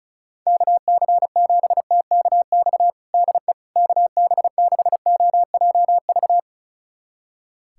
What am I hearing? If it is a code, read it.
Morse code, heard as a keyed tone, about 34 wpm, 700 Hz. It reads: KC7TKX DE KB6OJV
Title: USA Callsign DE Callsign - 6 Characters - 35wpm